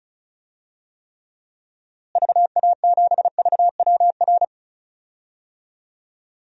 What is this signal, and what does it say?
Morse code, heard as a keyed tone, about 36 wpm, 700 Hz. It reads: VA7VWR
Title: DXCC-100 Call Signs - 6 Characters - Repeated 2x - 35wpm